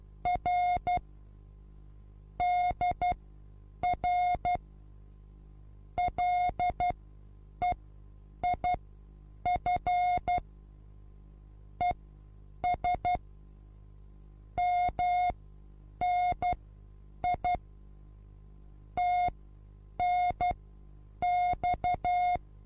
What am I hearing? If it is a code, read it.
Morse code, heard as a keyed tone, about 12 wpm, spaced as for 8 wpm, 720 Hz. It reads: R DR LEIF ES MNI TNX
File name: Onlinelehrgang / morsen / DJ4UF Morsen23